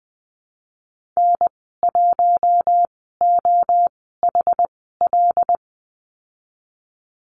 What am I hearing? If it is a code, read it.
Morse code, heard as a keyed tone, about 20 wpm, spaced as for 14 wpm, 700 Hz. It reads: N1OHL
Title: 186 Lesson 30g - Letter b - Call Signs 20wpm